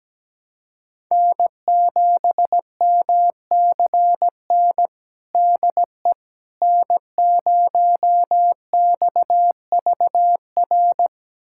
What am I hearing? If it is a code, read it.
Morse code, heard as a keyed tone, about 17 wpm, 700 Hz. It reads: N7MCN DE N0XVR
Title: USA Callsign DE Callsign - 5 Characters - 17wpm